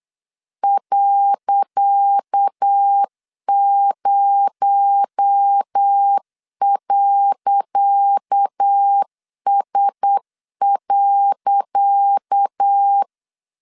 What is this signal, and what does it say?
Morse code, heard as a keyed tone, about 8 wpm, 800 Hz. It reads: .0.S.
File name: LEZIONI / 10 SIGNS / 10C DOT 8wpm 3649